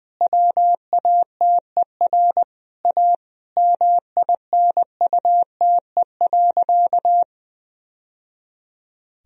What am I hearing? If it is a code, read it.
Morse code, heard as a keyed tone, about 20 wpm, 700 Hz. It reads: WATER A MINUTE.
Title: Sentences from Top 1000 Words - Review Part 1 - 20wpm